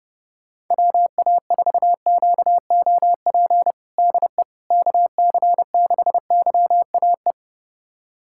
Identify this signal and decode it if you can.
Morse code, heard as a keyed tone, about 30 wpm, 700 Hz. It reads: WA4QOP DE KC6YAE
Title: USA Callsign DE Callsign 30wpm